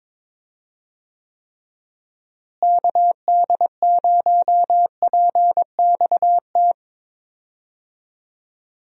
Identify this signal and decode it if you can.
Morse code, heard as a keyed tone, about 22 wpm, 700 Hz. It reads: KD0PXT
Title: USA Call Signs - 6 Characters - Repeated 2x - 22wpm